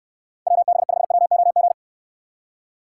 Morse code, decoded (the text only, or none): WB5WOG